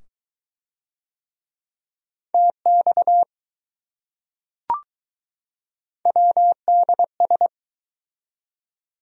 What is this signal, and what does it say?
Morse code, heard as a keyed tone, about 21 wpm, 700 Hz. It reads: TX WDS